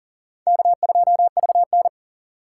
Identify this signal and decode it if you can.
Morse code, heard as a keyed tone, about 40 wpm, 700 Hz. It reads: K2VN
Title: USA Call Signs - 4 Characters - 3x Character Spacing - 40wpm